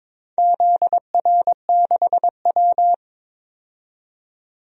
Morse code, heard as a keyed tone, about 22 wpm, 700 Hz. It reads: ZR6W
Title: DXCC-100 Call Signs - 4 Characters - 22wpm